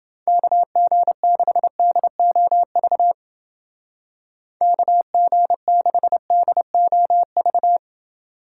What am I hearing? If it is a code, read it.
Morse code, heard as a keyed tone, about 29 wpm, 700 Hz. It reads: KG6DOV KG6DOV